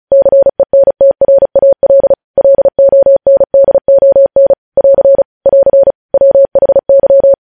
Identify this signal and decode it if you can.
Morse code, heard as a keyed tone, about 36 wpm, 550 Hz. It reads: CENTRAL LONDON <AR> <AR> WHY